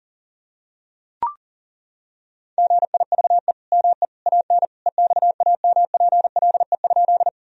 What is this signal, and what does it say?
Morse code, heard as a keyed tone, about 40 wpm, 700 Hz. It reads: GIVE ME AN EXAMPLE?
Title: Sentences from Top 1000 Words - Review Part 1 - 40wpm